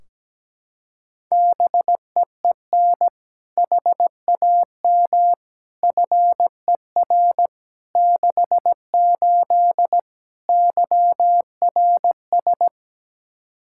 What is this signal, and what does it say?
Morse code, heard as a keyed tone, about 17 wpm, 700 Hz. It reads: BEEN HAM FER 68 YRS
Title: Practice QSO - Exchange 3 - More Info - 17wpm